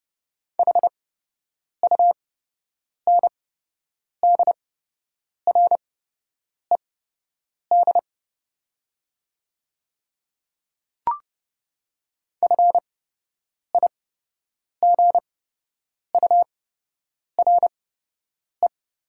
Morse code, heard as a keyed tone, about 30 wpm, spaced as for 8 wpm, 700 Hz. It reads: HUNDRED FIGURE
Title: 8x Character Spacing - Top 500 Words - 30wpm